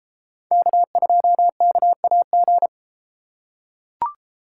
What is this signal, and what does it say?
Morse code, heard as a keyed tone, about 33 wpm, 700 Hz. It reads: K2KAG